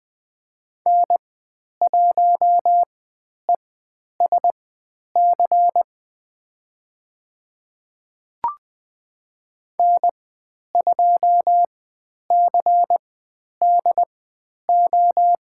Morse code, heard as a keyed tone, about 20 wpm, spaced as for 10 wpm, 700 Hz. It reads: N1ESC N2CDO
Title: 095 Lesson 16g - Letter c - Call Signs 20wpm